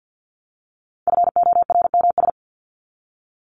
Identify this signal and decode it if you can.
Morse code, heard as a keyed tone, about 50 wpm, 700 Hz. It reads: FORMS
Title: ES:S Suffix - Top 100 Words with Base Words 50wpm